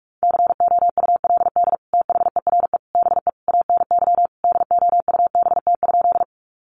Morse code, heard as a keyed tone, about 44 wpm, 700 Hz. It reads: COULD THERE BE ANY DOUBT?
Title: Sentences from Top 1500 Words 45wpm